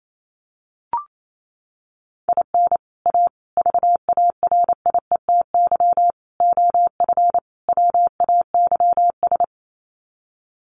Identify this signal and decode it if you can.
Morse code, heard as a keyed tone, about 28 wpm, 700 Hz. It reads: IN A VARIETY OF WAYS
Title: Sets of 5 Words Encore 28wpm